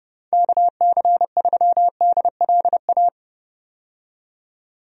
Morse code, heard as a keyed tone, about 32 wpm, 700 Hz. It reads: KC3DLA
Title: USA Call Signs - 6 Characters - 30wpm